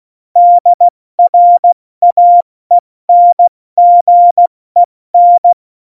Morse code, heard as a keyed tone, about 16 wpm, 700 Hz. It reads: DRAENGEN